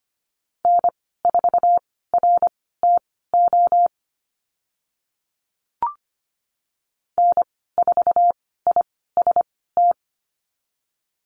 Morse code, heard as a keyed tone, about 25 wpm, spaced as for 16 wpm, 700 Hz. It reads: N4RTO N4IST
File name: Lesson 15g - Number 5 - Call Signs 25wpm